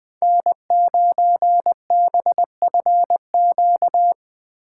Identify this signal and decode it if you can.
Morse code, heard as a keyed tone, about 20 wpm, 700 Hz. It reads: N9BFQ